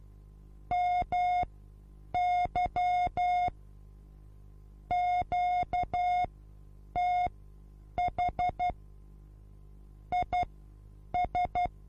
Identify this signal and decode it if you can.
Morse code, heard as a keyed tone, about 12 wpm, spaced as for 8 wpm, 720 Hz. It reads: MY QTH IS
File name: Onlinelehrgang / morsen / DJ4UF Morsen23